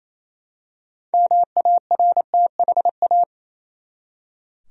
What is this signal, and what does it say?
Morse code, heard as a keyed tone, about 28 wpm, 700 Hz. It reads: MARTHA
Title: USA Common Names - 28wpm